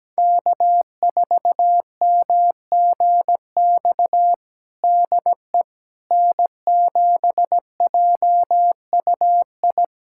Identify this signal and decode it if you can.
Morse code, heard as a keyed tone, about 17 wpm, 700 Hz. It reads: K4MGX DE N7JUI
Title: USA Callsign DE Callsign - 5 Characters - 17wpm